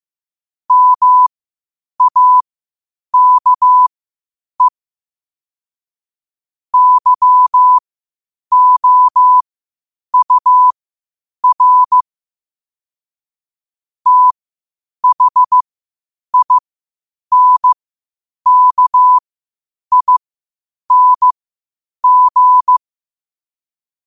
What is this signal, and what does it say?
Morse code, heard as a keyed tone, about 15 wpm, spaced as for 8 wpm, 1000 Hz. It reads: MAKE YOUR THINKING
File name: qotdfiles / 8wpm